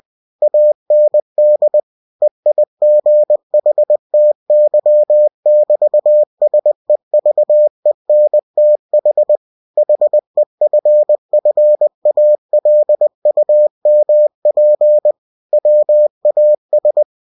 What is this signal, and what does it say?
Morse code, heard as a keyed tone, about 20 wpm, 600 Hz. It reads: AND EIGHTY<BT>SEVENTH HEFFALUMP WAS
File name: cw-books / winnie-the-pooh / winnie-20wpm-page-0040